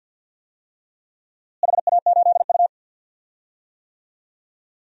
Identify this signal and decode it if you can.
Morse code, heard as a keyed tone, about 52 wpm, 700 Hz. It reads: HA8U
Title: DXCC-100 Call Signs - 4 Characters - Repeated 2x - 50wpm